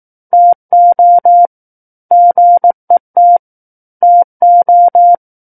Morse code, heard as a keyed tone, about 18 wpm, 700 Hz. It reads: TO GET TO
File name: 18 WPM